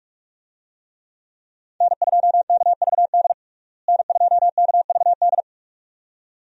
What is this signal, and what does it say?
Morse code, heard as a keyed tone, about 46 wpm, 700 Hz. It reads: N2KVD N2KVD
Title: USA Call Signs - 5 Characters - Repeated 2x - 45wpm